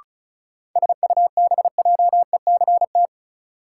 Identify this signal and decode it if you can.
Morse code, heard as a keyed tone, about 36 wpm, 700 Hz. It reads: SUBJECT
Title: Sentences from Top 800 Words - New Word Drill - 35wpm